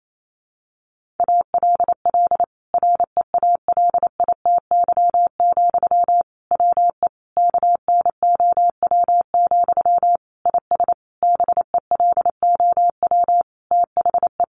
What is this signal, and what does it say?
Morse code, heard as a keyed tone, about 28 wpm, 700 Hz. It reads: ALL REALITY, WE KNOW, IS BELOW THE
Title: Sentences from Top 1400 Words 28wpm